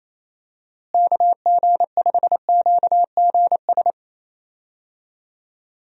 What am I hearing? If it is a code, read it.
Morse code, heard as a keyed tone, about 28 wpm, 700 Hz. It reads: KG5QGS
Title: USA Call Signs - 6 Characters - Repeated 3x - 28wpm